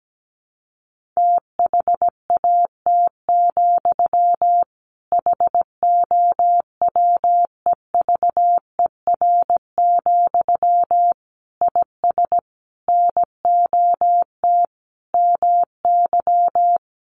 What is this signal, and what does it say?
Morse code, heard as a keyed tone, about 17 wpm, 700 Hz. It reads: THAT, HOWEVER, IS NOT MY
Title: Sentences from Top 1500 Words 17wpm